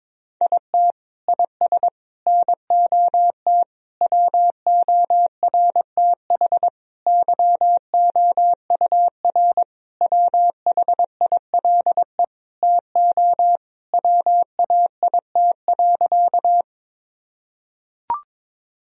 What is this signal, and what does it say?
Morse code, heard as a keyed tone, about 22 wpm, 700 Hz. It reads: IT IS NOT WORTH YOUR WHILE TO WAIT.
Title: Sentences from Top 1000 Words - Review Part 1 - 22wpm